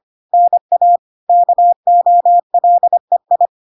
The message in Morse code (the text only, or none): NA KOLEI